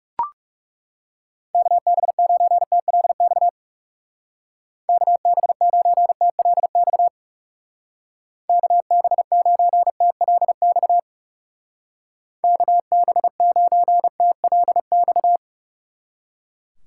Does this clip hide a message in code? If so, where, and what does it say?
Yes: Morse code from 1.5 to 15.4 s, heard as a keyed tone, about 39 wpm, 700 Hz. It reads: KB9TLX KB9TLX KB9TLX KB9TLX